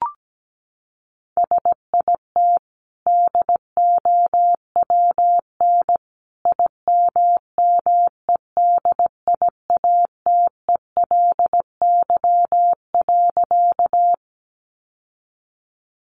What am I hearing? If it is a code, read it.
Morse code, heard as a keyed tone, about 17 wpm, 700 Hz. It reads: SIT DOWN IMMEDIATELY.
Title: Sentences from Top 1100 Words 17wpm